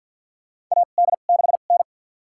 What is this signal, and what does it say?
Morse code, heard as a keyed tone, about 50 wpm, spaced as for 36 wpm, 700 Hz. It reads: AD6N
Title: USA Call Signs - 4 Characters - 2x Character Spacing - 50wpm